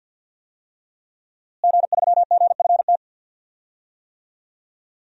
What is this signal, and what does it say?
Morse code, heard as a keyed tone, about 51 wpm, 700 Hz. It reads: G3GFT
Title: DXCC-100 Call Signs - 5 Characters - Repeated 2x - 50wpm